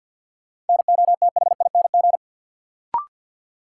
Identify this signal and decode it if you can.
Morse code, heard as a keyed tone, about 51 wpm, 700 Hz. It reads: NOTHING
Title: Sets of 4 Words - New Word Drill - 50wpm